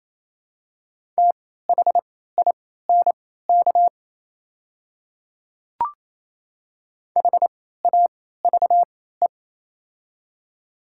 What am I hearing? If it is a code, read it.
Morse code, heard as a keyed tone, about 28 wpm, spaced as for 16 wpm, 700 Hz. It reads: THINK HAVE
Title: 3x Character Spacing - Top 300 Words - 28wpm